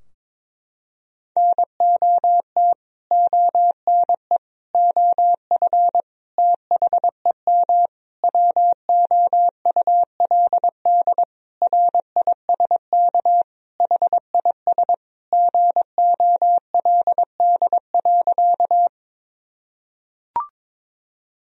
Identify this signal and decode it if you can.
Morse code, heard as a keyed tone, about 22 wpm, 700 Hz. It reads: NOT ONE OF THEM WOULD RISK HIS GOLD.
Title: Sentences from Top 1000 Words - Review Part 4 - 22wpm